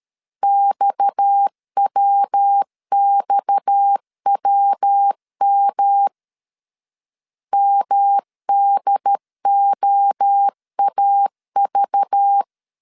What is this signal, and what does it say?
Morse code, heard as a keyed tone, about 13 wpm, 800 Hz. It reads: XWXWM MDOAV
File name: CW / DET / 1 6 1500C S B 12wpm 3431